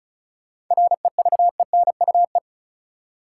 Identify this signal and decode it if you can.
Morse code, heard as a keyed tone, about 34 wpm, 700 Hz. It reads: REVENUE
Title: Seven Letter Words - 35wpm